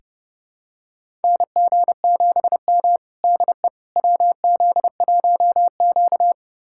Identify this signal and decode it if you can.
Morse code, heard as a keyed tone, about 32 wpm, 700 Hz. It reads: NG7M DE WZ1Q
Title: USA Callsign DE Callsign - 4 Characters - 30wpm